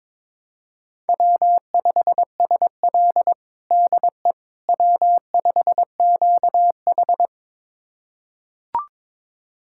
Morse code, heard as a keyed tone, about 22 wpm, 700 Hz. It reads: W5SL DE W5QH
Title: USA Callsign DE Callsign - 4 Characters - 22wpm